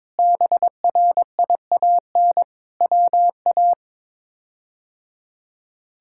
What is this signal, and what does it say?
Morse code, heard as a keyed tone, about 22 wpm, 700 Hz. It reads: BRIAN WA
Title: K1USN SST QSO Practice - 22wpm